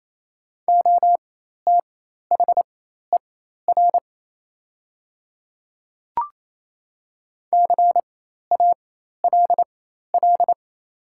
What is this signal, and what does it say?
Morse code, heard as a keyed tone, about 28 wpm, spaced as for 13 wpm, 690 Hz. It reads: OTHER CALL
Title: 4x Character Spacing - Top 200 Words - 28wpm